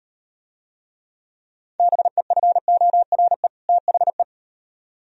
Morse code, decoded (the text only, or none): BEFORE THE